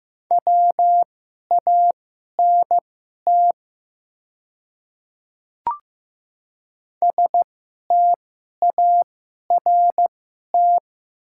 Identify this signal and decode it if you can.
Morse code, heard as a keyed tone, about 15 wpm, spaced as for 11 wpm, 700 Hz. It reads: WANT START
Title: 2x Character Spacing - Top 300 Words - 15wpm